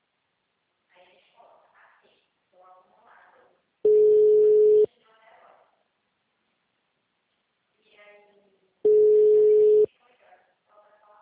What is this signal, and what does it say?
Ringback tone (Europe, 425 Hz), with speech.